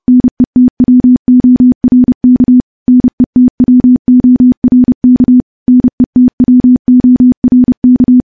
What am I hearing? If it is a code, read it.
Morse code, heard as a keyed tone, about 28 wpm, 250 Hz. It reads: NETWORK NETWORK NETWORK
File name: morse / morse-training-30WPM